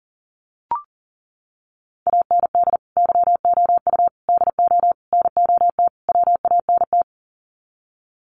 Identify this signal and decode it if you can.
Morse code, heard as a keyed tone, about 40 wpm, 700 Hz. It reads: AND YOU DO NOT WANT
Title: Sets of 5 Words Encore2 40wpm